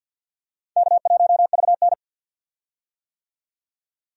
Morse code, heard as a keyed tone, about 47 wpm, 700 Hz. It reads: K1VN
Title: USA Call Signs - 4 Characters - 50wpm